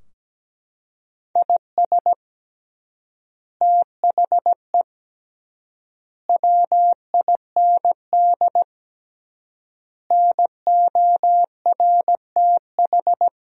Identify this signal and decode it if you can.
Morse code, heard as a keyed tone, about 17 wpm, 700 Hz. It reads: IS THE WIND NORTH
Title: 3x Word-Spacing - Sentences from Top 400 Words - 17wpm